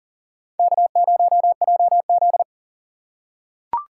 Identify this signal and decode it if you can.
Morse code, heard as a keyed tone, about 40 wpm, 700 Hz. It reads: K0JZ
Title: USA Call Signs - 4 Characters - 6x Character Spacing - 40wpm